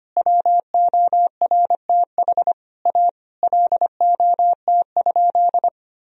Morse code, heard as a keyed tone, about 25 wpm, 700 Hz. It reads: WORTH A LOT?
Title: Sentences from Top 1000 Words - Review Part 4 - 25wpm